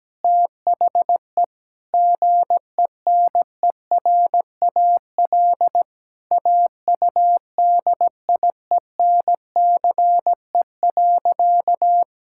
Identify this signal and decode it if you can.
Morse code, heard as a keyed tone, about 17 wpm, 700 Hz. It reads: THE GENERAL AUDIENCE.
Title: Sentences from Top 1000 Words - Review Part 5 - 17wpm